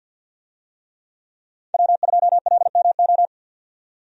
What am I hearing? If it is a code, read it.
Morse code, heard as a keyed tone, about 52 wpm, 700 Hz. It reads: W2LMO
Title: USA Call Signs - 5 Characters - 50wpm